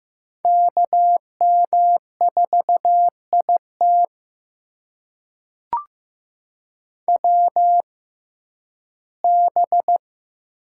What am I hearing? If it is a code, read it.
Morse code, heard as a keyed tone, about 15 wpm, spaced as for 11 wpm, 700 Hz. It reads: KM4IT WB
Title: USA Call Signs - 6x Character Spacing - 15wpm